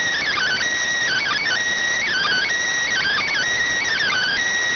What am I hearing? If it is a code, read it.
Emergency Alert System (SAME) tones, heard as harsh buzzing bursts. It is heard from 0.2 to 4.4 s.